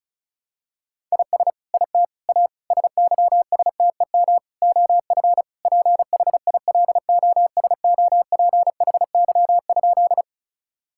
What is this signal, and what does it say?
Morse code, heard as a keyed tone, about 36 wpm, 700 Hz. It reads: IS IT A SYSTEM OF PHILOSOPHY?
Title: Sentences from Top 1000 Words - Review Part 6 - 35wpm